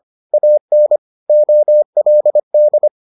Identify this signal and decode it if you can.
Morse code, heard as a keyed tone, about 25 wpm, 600 Hz. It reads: AN OLD